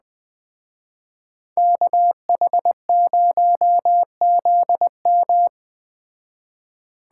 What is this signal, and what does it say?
Morse code, heard as a keyed tone, about 20 wpm, 700 Hz. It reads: KH0ZM
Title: Speed Racing - USA 2x2 Call Signs 17wpm